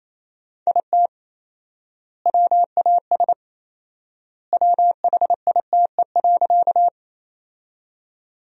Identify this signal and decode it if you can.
Morse code, heard as a keyed tone, about 28 wpm, 700 Hz. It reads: IT WAS WHITE.
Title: 4x Word-Spacing - Sentences from Top 300 Words - 28wpm